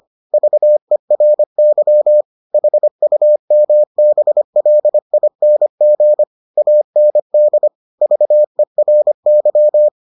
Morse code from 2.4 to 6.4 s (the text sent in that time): HUMBLING